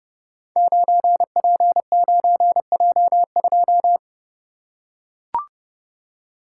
Morse code, heard as a keyed tone, about 30 wpm, 700 Hz. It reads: 9P9J2
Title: Code Groups - 5 Letter:Numbers - 30wpm